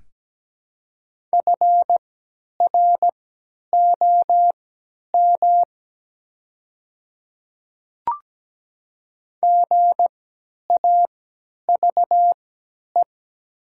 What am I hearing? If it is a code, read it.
Morse code, heard as a keyed tone, about 17 wpm, spaced as for 10 wpm, 700 Hz. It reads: FROM GAVE
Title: 3x Character Spacing - Top 500 Words - 17wpm